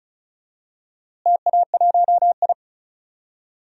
Morse code, heard as a keyed tone, about 34 wpm, 700 Hz. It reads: TA1I
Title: DXCC-100 Call Signs - 4 Characters - Repeated 3x - 35wpm